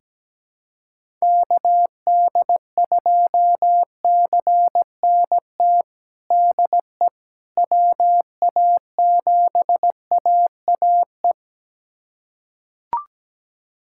Morse code, heard as a keyed tone, about 17 wpm, 700 Hz. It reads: KD2CNT DE WA7AAE
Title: USA Callsign DE Callsign - 6 Characters - 17wpm